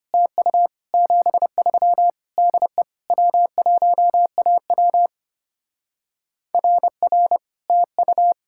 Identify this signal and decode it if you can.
Morse code, heard as a keyed tone, about 30 wpm, 700 Hz. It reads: TU 73 DE W1AW RR TU
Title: Speed-Racing Common QSO Phrases Encore - 25wpm